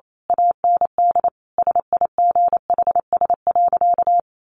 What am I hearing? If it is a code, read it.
Morse code, heard as a keyed tone, about 28 wpm, 700 Hz. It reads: AND SIGHS.
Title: A Sound Of Thunder 28wpm